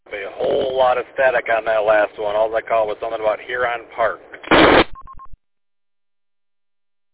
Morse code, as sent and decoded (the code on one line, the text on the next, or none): none